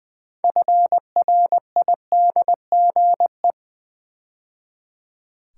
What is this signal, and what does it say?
Morse code, heard as a keyed tone, about 20 wpm, 700 Hz. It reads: FRIDGE
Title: 6 Letter Words - First 3 Letters Given - 20wpm